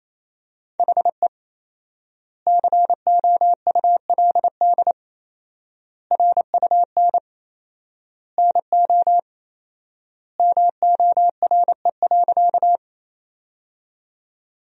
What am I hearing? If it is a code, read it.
Morse code, heard as a keyed tone, about 28 wpm, 700 Hz. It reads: HE COULD RUN NO MORE.
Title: 4x Word-Spacing - Sentences from Top 300 Words - 28wpm